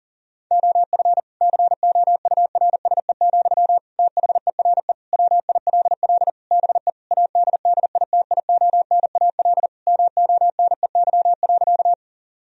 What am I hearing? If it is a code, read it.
Morse code, heard as a keyed tone, about 40 wpm, 700 Hz. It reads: OF COURSE, THERE WILL BE ADDITIONAL MONEY.
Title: Sentences from Top 1000 Words - Review Part 4 - 40wpm